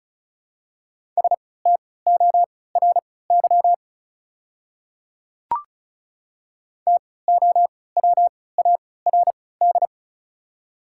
Morse code, heard as a keyed tone, about 36 wpm, spaced as for 20 wpm, 700 Hz. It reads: STORY TOWARD
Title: 3x Character Spacing - Top 500 Words - 35wpm